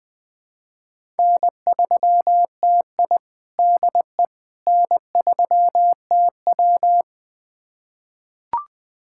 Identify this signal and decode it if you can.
Morse code, heard as a keyed tone, about 20 wpm, 700 Hz. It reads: N3TI DE N3TW
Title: USA Callsign DE Callsign - 4 Characters - 20wpm